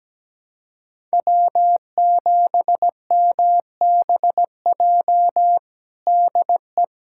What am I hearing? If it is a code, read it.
Morse code, heard as a keyed tone, about 17 wpm, 700 Hz. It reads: W7MBJ DE
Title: USA Callsign DE Callsign - 5 Characters - 17wpm